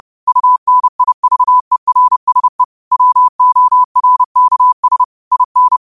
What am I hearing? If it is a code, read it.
Morse code, heard as a keyed tone, about 30 wpm, 1000 Hz. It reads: UNIVERSE WORKS IN
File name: qotdfiles / 30wpm